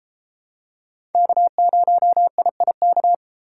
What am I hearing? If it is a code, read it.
Morse code, heard as a keyed tone, about 31 wpm, 700 Hz. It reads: K0IIK